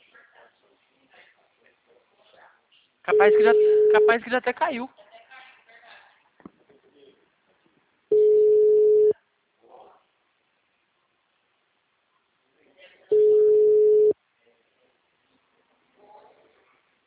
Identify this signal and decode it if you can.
Ringback tone (Europe, 425 Hz), with speech.